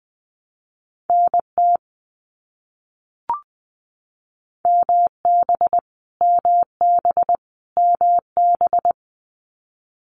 Morse code, heard as a keyed tone, about 20 wpm, 700 Hz. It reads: NT MB MB MB
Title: Canadian Provinces and Territories - Repeated 3x - 20wpm